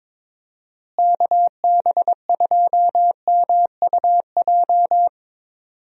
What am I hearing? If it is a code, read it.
Morse code, heard as a keyed tone, about 22 wpm, 700 Hz. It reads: KB2MUJ
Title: USA Call Signs - 6 Characters - Repeated 3x - 22wpm